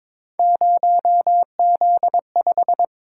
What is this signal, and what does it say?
Morse code, heard as a keyed tone, about 22 wpm, 700 Hz. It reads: 0Z5
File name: practice / 22wpm / Three Letter-Number Combinations - 22wpm